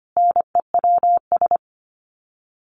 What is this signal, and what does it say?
Morse code, heard as a keyed tone, about 25 wpm, 700 Hz. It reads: NEWS
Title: Sets of 5 Words Encore 25wpm